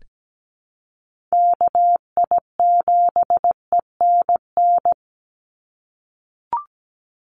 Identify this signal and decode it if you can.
Morse code, heard as a keyed tone, about 17 wpm, 700 Hz. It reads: KI7ENN
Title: USA Call Signs 17wpm